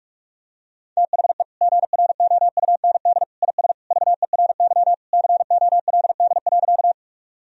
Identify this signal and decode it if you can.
Morse code, heard as a keyed tone, about 41 wpm, 700 Hz. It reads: THE GROUND IS VERY COLD.
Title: Sentences from Top 1000 Words - Review Part 2 - 45wpm